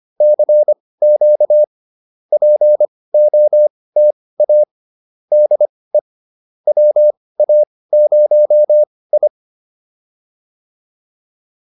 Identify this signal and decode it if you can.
Morse code, heard as a keyed tone, about 25 wpm, spaced as for 18 wpm, 600 Hz. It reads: CQ POTA DE WA0I